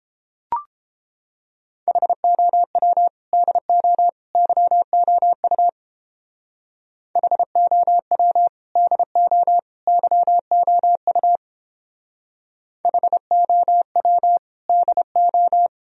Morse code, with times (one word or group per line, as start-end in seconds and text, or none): 1.9-3.1 HOW
3.3-4.1 DO
4.3-5.7 YOU
7.2-8.5 HOW
8.8-9.6 DO
9.9-11.4 YOU
12.8-14.4 HOW
14.7-15.7 DO